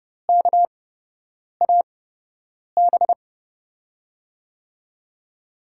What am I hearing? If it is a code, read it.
Morse code, heard as a keyed tone, about 30 wpm, spaced as for 8 wpm, 700 Hz. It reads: KAB
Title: USA Call Signs - 6 Characters - 8x Character Spacing - 30wpm